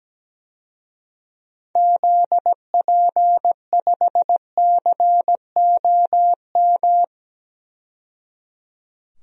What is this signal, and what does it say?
Morse code, heard as a keyed tone, about 17 wpm, 700 Hz. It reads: ZP5COM
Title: DXCC-100 Call Signs - 6 Characters - 17wpm